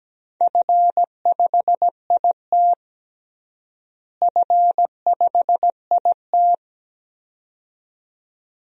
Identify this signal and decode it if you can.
Morse code, heard as a keyed tone, about 17 wpm, 700 Hz. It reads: F5IT F5IT
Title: DXCC-100 Call Signs - 4 Characters - Repeated 2x - 17wpm